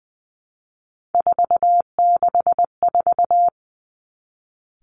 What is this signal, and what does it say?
Morse code, heard as a keyed tone, about 20 wpm, 700 Hz. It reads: 464
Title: Three Challenging Letter-Number Combinations 20wpm